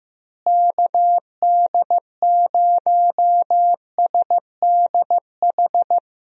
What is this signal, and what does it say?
Morse code, heard as a keyed tone, about 15 wpm, 700 Hz. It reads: KD0SDH
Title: USA Call Signs - 6 Characters - Repeated 3x - 15wpm